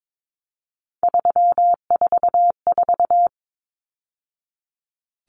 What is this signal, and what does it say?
Morse code, heard as a keyed tone, about 22 wpm, 700 Hz. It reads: 344
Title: Numbers - 3-Digit 22wpm